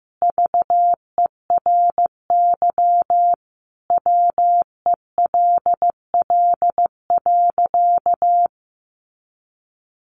Morse code, heard as a keyed tone, about 15 wpm, 700 Hz. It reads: VERY WELL.
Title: A Sound Of Thunder 15wpm